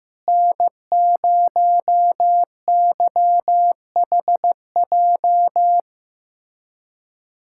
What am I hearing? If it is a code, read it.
Morse code, heard as a keyed tone, about 15 wpm, 700 Hz. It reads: N0YHJ
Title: USA Callsign DE Callsign - 5 Characters - 15wpm